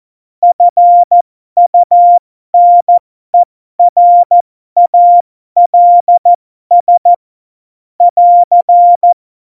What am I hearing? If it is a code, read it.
Morse code, heard as a keyed tone, about 14 wpm, spaced as for 12 wpm, 700 Hz. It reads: FUNERALS <AR>
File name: Morsekurs / Witze